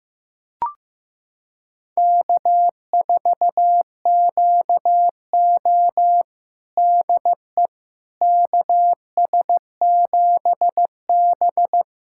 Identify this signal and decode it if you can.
Morse code, heard as a keyed tone, about 15 wpm, 700 Hz. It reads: K4QO DE KS7B
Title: USA Callsign DE Callsign - 4 Characters - 15wpm